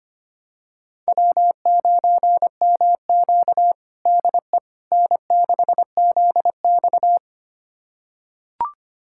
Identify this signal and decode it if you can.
Morse code, heard as a keyed tone, about 25 wpm, 700 Hz. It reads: W9MQ DE N6ZX
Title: USA Callsign DE Callsign - 4 Characters - 25wpm